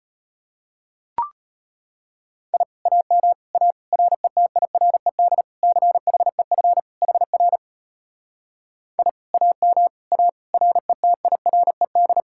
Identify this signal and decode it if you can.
Morse code, heard as a keyed tone, about 33 wpm, 700 Hz. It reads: I AM A RETIRED CHEF HR I AM A RETIRED